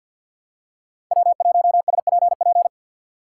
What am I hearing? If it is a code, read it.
Morse code, heard as a keyed tone, about 53 wpm, 700 Hz. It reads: W1SPP
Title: USA Call Signs - 5 Characters - 4x Character Spacing - 50wpm